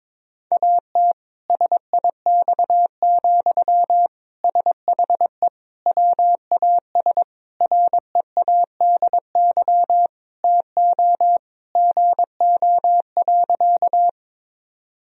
Morse code, heard as a keyed tone, about 22 wpm, 700 Hz. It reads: AT SIX, SHE WAS READY TO GO.
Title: Sentences from Top 1000 Words - Review Part 2 - 22wpm